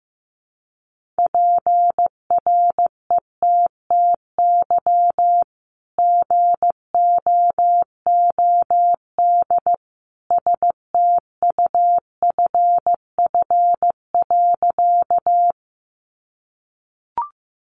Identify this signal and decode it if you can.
Morse code, heard as a keyed tone, about 15 wpm, 700 Hz. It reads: PRETTY GOOD STUFF.
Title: Sentences from Top 1400 Words 15wpm